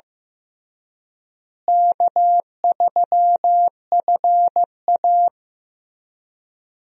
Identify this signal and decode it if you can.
Morse code, heard as a keyed tone, about 16 wpm, 700 Hz. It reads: K3FA
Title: USA Call Signs - 4 Characters - Repeated 3x - 15wpm